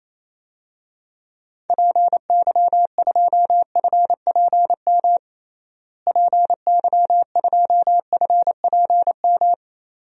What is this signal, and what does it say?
Morse code, heard as a keyed tone, about 28 wpm, 700 Hz. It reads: PY2FPM PY2FPM
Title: DXCC-100 Call Signs - 6 Characters - Repeated 2x - 28wpm